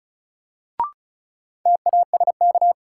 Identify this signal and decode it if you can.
Morse code, heard as a keyed tone, about 36 wpm, 700 Hz. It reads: TASK
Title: Sentences from Top 1100 Words - New Word Drill Alt - 35wpm